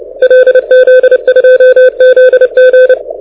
Morse code, heard as a keyed tone, about 28 wpm, 500 Hz. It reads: LZ2ZG
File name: audio zawody / IARU 2015 CW SN7Q